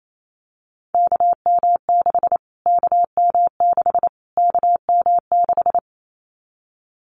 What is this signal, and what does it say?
Morse code, heard as a keyed tone, about 28 wpm, 700 Hz. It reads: KM6 KM6 KM6
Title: Callsign Prefixes for Top 25 DXCC - Repeated 3x - 28wpm